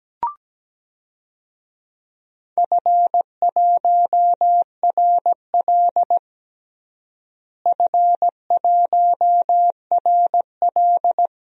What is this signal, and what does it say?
Morse code, heard as a keyed tone, about 17 wpm, 700 Hz. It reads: F1RL F1RL
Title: DXCC-100 Call Signs - 4 Characters - Repeated 2x - 17wpm